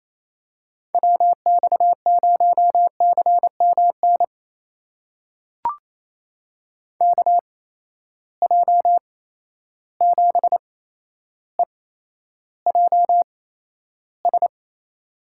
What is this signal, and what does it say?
Morse code, heard as a keyed tone, about 28 wpm, spaced as for 12 wpm, 700 Hz. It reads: WX0CMN KJ7EJS
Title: USA Call Signs - 6 Characters - 8x Character Spacing - 28wpm